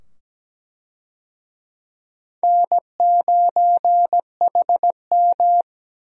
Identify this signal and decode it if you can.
Morse code, heard as a keyed tone, about 17 wpm, 700 Hz. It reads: N9HM
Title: USA Call Signs - 4 Characters - 17wpm